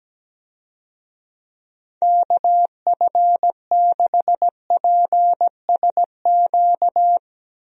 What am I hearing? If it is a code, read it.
Morse code, heard as a keyed tone, about 17 wpm, 700 Hz. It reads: KF6PSQ
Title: USA Call Signs - 6 Characters - Repeated 2x - 17wpm